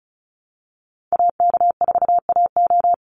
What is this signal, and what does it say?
Morse code, heard as a keyed tone, about 34 wpm, 700 Hz. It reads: AK4AO
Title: Call Signs Repeated 3x - 35wpm